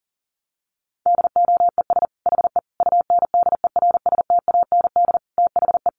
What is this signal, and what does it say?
Morse code, heard as a keyed tone, about 40 wpm, 700 Hz. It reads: DOES HE UNDERSTAND THE
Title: Sentences from Top 1100 Words 40wpm